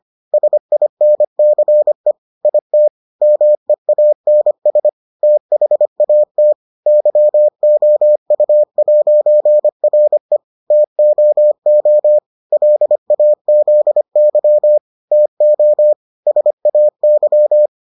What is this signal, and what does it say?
Morse code, heard as a keyed tone, about 25 wpm, 600 Hz. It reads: SINCE IT MEANS THAT YOU'RE TOO LAZY TO SAY